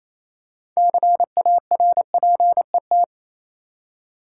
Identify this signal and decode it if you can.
Morse code, heard as a keyed tone, about 28 wpm, 700 Hz. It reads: CARPET
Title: Six Letter Words - 28wpm